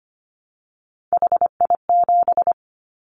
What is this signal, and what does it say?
Morse code, heard as a keyed tone, about 25 wpm, 700 Hz. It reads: HI7
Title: Callsign Prefixes for Top 100 DXCC - 25wpm